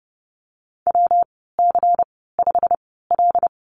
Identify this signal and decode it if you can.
Morse code, heard as a keyed tone, about 30 wpm, spaced as for 17 wpm, 700 Hz. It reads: WC5L
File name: Lesson 19g - Letter w - Call Signs 30wpm